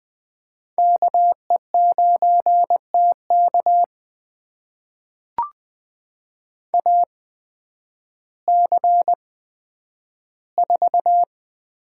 Morse code, heard as a keyed tone, about 20 wpm, spaced as for 11 wpm, 700 Hz. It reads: KE9TK AC4
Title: USA Call Signs - 5 Characters - 8x Character Spacing - 20wpm